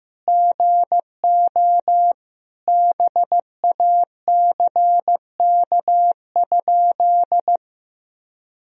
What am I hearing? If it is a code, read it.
Morse code, heard as a keyed tone, about 15 wpm, 700 Hz. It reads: GO BACK?
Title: Sentences from Top 1000 Words - Review Part 4 - 15wpm